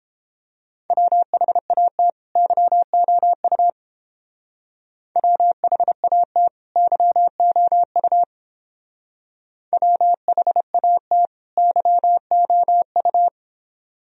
Morse code, heard as a keyed tone, about 31 wpm, 700 Hz. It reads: WHAT YOU WHAT YOU WHAT YOU